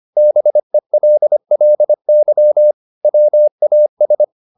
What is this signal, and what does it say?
Morse code, heard as a keyed tone, about 25 wpm, 600 Hz. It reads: BELLY WAS